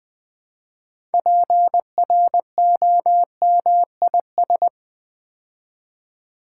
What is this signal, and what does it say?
Morse code, heard as a keyed tone, about 20 wpm, 700 Hz. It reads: PROMIS
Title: Build A Word - 3 Letters At A Time - 9-Letter Words - 20wpm